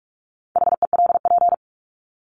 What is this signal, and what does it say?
Morse code, heard as a keyed tone, about 44 wpm, 700 Hz. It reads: HELP
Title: Top 200 Words 45wpm